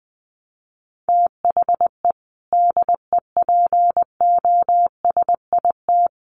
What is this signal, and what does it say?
Morse code, heard as a keyed tone, about 20 wpm, 700 Hz. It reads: THE DEPOSIT
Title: Sentences from Top 1300 Words 20wpm